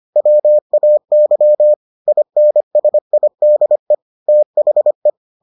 Morse code, heard as a keyed tone, about 25 wpm, 600 Hz. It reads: WAY INSIDE THE